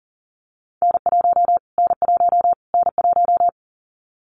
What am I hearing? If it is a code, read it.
Morse code, heard as a keyed tone, about 40 wpm, 700 Hz. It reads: N1 N1 N1
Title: Callsign Prefixes for Top 25 DXCC - Repeated 3x - 40wpm